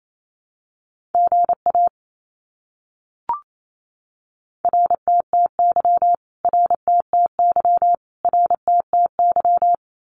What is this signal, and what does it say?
Morse code, heard as a keyed tone, about 27 wpm, 700 Hz. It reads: GA RTTY RTTY RTTY
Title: Abbreviations Repeated 3x - 28wpm